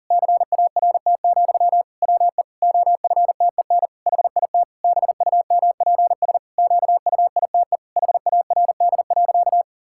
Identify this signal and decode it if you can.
Morse code, heard as a keyed tone, about 40 wpm, 700 Hz. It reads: CART, WE OFTEN HIT BUMPS QUITE HARD.
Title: Pangram Challenge v2 - 40wpm